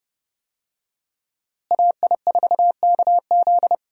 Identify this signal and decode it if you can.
Morse code, heard as a keyed tone, about 32 wpm, 700 Hz. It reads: AI4KZ
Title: USA Call Signs - 5 Characters - Repeated 3x - 30wpm